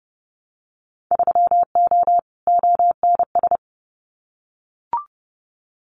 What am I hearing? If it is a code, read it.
Morse code, heard as a keyed tone, about 31 wpm, 700 Hz. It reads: 3O ONS
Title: Winter Field Day Exchanges 30wpm